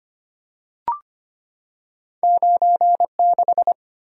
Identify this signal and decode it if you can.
Morse code, heard as a keyed tone, about 25 wpm, 700 Hz. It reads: 96
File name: Two Letter-Number Combinations - 25wpm